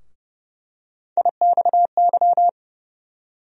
Morse code, heard as a keyed tone, about 30 wpm, 700 Hz. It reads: IXY
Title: USA Call Sign Postfixes - 3 Letters 30wpm